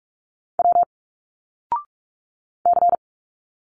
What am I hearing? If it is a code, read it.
Morse code, heard as a keyed tone, about 46 wpm, 700 Hz. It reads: W C
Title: Single Letters 45wpm